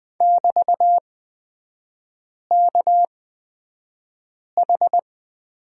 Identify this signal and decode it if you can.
Morse code, heard as a keyed tone, about 20 wpm, spaced as for 5 wpm, 700 Hz. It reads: <BT>KH
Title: 224 Lesson 36e - BT - Three Characters 20wpm